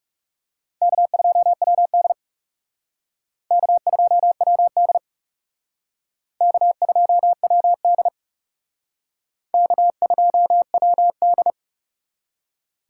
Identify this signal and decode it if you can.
Morse code, heard as a keyed tone, about 38 wpm, 700 Hz. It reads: K2WD K2WD K2WD K2WD